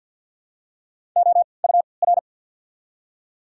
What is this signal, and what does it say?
Morse code, heard as a keyed tone, about 54 wpm, spaced as for 29 wpm, 700 Hz. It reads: OUR
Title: 3x Character Spacing - Top 300 Words - 50wpm